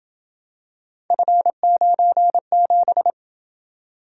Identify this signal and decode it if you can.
Morse code, heard as a keyed tone, about 27 wpm, 700 Hz. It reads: F97